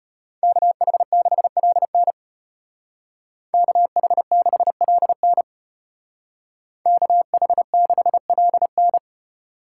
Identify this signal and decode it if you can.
Morse code, heard as a keyed tone, about 31 wpm, 700 Hz. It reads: KH6LN KH6LN KH6LN
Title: Speed Racing - USA 2x2 Call Signs 25wpm